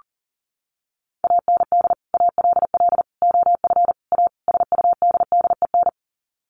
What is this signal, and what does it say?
Morse code, heard as a keyed tone, about 40 wpm, 700 Hz. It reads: AND ALL OF A SUDDEN
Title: Sets of 5 Words 40wpm